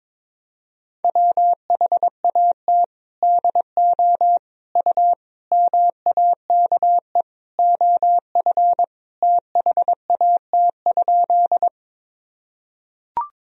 Morse code, heard as a keyed tone, about 22 wpm, 700 Hz. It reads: WHAT DO U MAKE OF THAT?
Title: Sentences from Top 100 Words and Abbreviations 22wpm